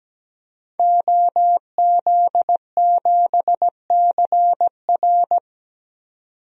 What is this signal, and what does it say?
Morse code, heard as a keyed tone, about 17 wpm, 700 Hz. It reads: OZ7CR
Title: DXCC-100 Call Signs - 5 Characters - 17wpm